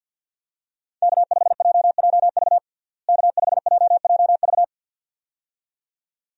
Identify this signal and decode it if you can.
Morse code, heard as a keyed tone, about 52 wpm, 700 Hz. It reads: K5JJV K5JJV
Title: USA Call Signs - 5 Characters - Repeated 2x - 50wpm